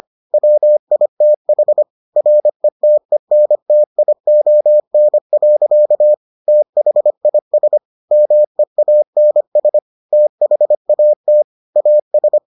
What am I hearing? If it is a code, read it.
Morse code, heard as a keyed tone, about 25 wpm, 600 Hz. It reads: WITH RETENTION. THIS MEANS THAT AS